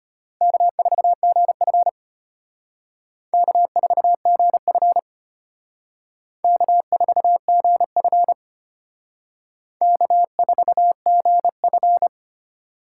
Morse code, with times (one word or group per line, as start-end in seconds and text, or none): 0.4-1.9 K4GF
3.3-5.0 K4GF
6.4-8.3 K4GF
9.8-12.1 K4GF